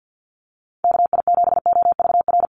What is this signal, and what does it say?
Morse code, heard as a keyed tone, about 50 wpm, 700 Hz. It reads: KI7OVR
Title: USA Call Signs Repeated 3x 50wpm